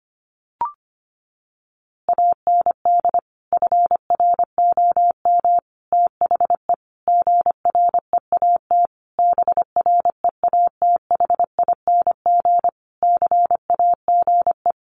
Morse code, heard as a keyed tone, about 25 wpm, 700 Hz. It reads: AND FROM THE GREAT BREATHING CAGE
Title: A Sound Of Thunder 25wpm